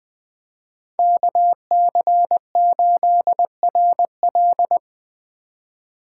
Morse code, heard as a keyed tone, about 24 wpm, 700 Hz. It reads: KC8RL
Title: Speed Racing - USA 2x2 Call Signs 20wpm